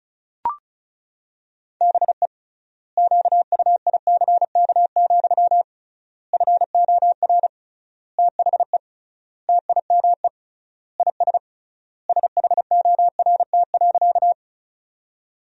Morse code, heard as a keyed tone, about 36 wpm, 700 Hz. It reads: BE QUICK, FOR THE TIME IS SHORT.
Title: 3x Word-Spacing - Sentences from Top 500 Words - 35wpm